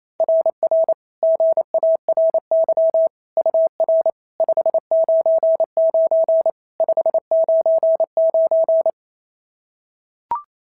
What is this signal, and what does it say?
Morse code, heard as a keyed tone, about 25 wpm, 650 Hz. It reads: RR GARY UR 599 599
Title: SOTA QSO Practice 28wpm